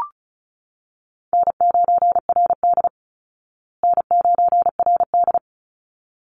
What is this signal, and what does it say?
Morse code, heard as a keyed tone, about 34 wpm, 700 Hz. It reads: N9RD N9RD
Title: Easy Call Signs Repeated 2x - 35wpm